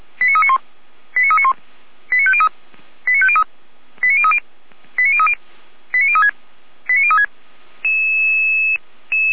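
Five-tone selective call (ZVEI), heard as quick runs of five tones, tones of about 70 ms. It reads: 89391 89391 89503 89503 89030 89030 89036 89036
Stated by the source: ZVEI five-tone selective call received on SDR.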